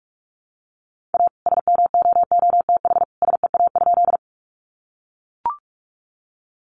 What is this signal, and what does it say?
Morse code, heard as a keyed tone, about 44 wpm, 700 Hz. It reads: A SMOOTH SEA?
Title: Sentences from Top 1400 Words 45wpm